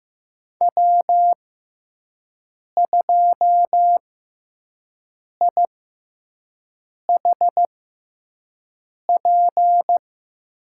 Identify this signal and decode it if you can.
Morse code, heard as a keyed tone, about 15 wpm, spaced as for 5 wpm, 700 Hz. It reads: W2IHP
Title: USA Call Signs - 5 Characters - 6x Character Spacing - 15wpm